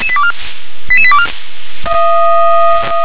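Five-tone selective call (ZVEI), heard as quick runs of five tones, tones of about 70 ms. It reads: 88824
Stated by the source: ZVEI five-tone selective call received by SDR.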